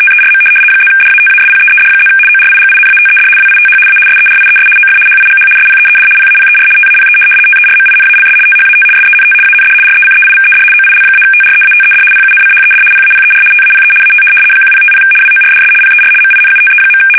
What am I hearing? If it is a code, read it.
RTTY, about 75 baud. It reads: TERM +DATASPEED+ ORIGINATED WITH A SERIES OF HIGH SPEED PAPER TAPE TERMINALS THAT SENT AND RECEIVED OIL-TREATED PUNCHED PAPER TAPE AT 1050 WORDS PER MINUTE.
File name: fsk850-75B / teletype-corp